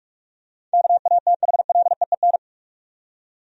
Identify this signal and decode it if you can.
Morse code, heard as a keyed tone, about 43 wpm, 700 Hz. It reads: KATHLEEN
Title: USA Common Names - 45wpm